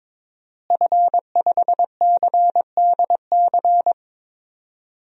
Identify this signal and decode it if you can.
Morse code, heard as a keyed tone, about 22 wpm, 700 Hz. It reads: F5CDC
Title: DXCC-100 Call Signs - 5 Characters - Repeated 2x - 22wpm